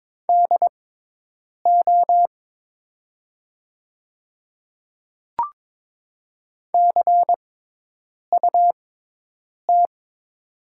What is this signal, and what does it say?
Morse code, heard as a keyed tone, about 22 wpm, spaced as for 8 wpm, 700 Hz. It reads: DO CUT
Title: 6x Character Spacing - Top 300 Words - 22wpm